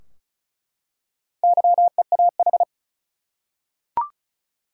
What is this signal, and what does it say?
Morse code, heard as a keyed tone, about 36 wpm, 700 Hz. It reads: YEAH
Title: Four Letter Words - 35wpm